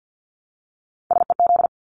Morse code, heard as a keyed tone, about 50 wpm, 700 Hz. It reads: SEZ
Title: Abbreviations 50wpm